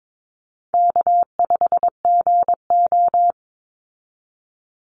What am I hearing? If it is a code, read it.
Morse code, heard as a keyed tone, about 22 wpm, 700 Hz. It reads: K5GO
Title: Easy Call Signs Repeated 2x - 22wpm